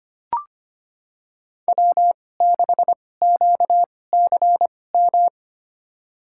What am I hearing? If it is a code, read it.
Morse code, heard as a keyed tone, about 25 wpm, spaced as for 18 wpm, 700 Hz. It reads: W6QCM
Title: USA Call Signs - 5 Characters - 2x Character Spacing - 25wpm